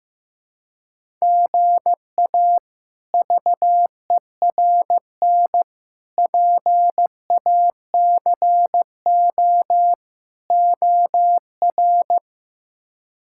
Morse code, heard as a keyed tone, about 16 wpm, 700 Hz. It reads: GA VERN PACO OR
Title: K1USN SST QSO Practice - 15wpm